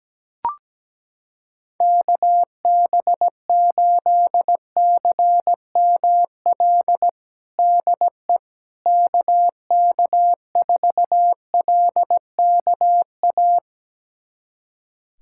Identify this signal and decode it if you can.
Morse code, heard as a keyed tone, about 17 wpm, 700 Hz. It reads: KB8CML DE KK4LKA
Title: USA Callsign DE Callsign 17wpm